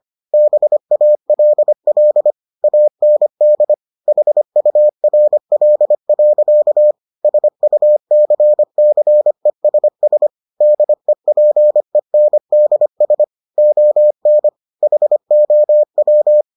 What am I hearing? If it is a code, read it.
Morse code, heard as a keyed tone, about 25 wpm, 600 Hz. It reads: BALL AND HURL. SUCCESS DEPENDS ON HOW